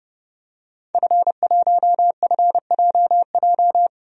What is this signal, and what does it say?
Morse code, heard as a keyed tone, about 30 wpm, 700 Hz. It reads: F1FJJ
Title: DXCC-100 Call Signs - 5 Characters - 30wpm